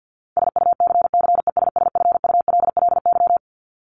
Morse code, heard as a keyed tone, about 50 wpm, 700 Hz. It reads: SUCCESSFULLY
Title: LY Suffix - Top 100 Words with Base Words 50wpm